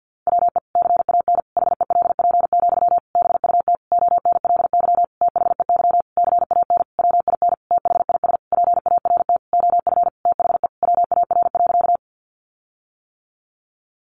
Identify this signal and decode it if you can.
Morse code, heard as a keyed tone, about 50 wpm, 700 Hz. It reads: WE CAN HELP, BUT ONLY THEY CAN WIN THIS PART OF THE WAR.
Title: Sentences from Top 1500 Words 50wpm